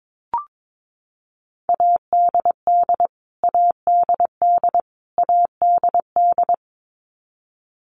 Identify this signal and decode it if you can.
Morse code, heard as a keyed tone, about 22 wpm, 700 Hz. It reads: ADD ADD ADD
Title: Top 200 Words Repeated 22wpm